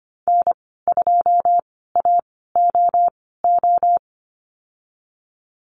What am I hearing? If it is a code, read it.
Morse code, heard as a keyed tone, about 25 wpm, spaced as for 16 wpm, 700 Hz. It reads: N2AOO
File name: Lesson 15g - Number 5 - Call Signs 25wpm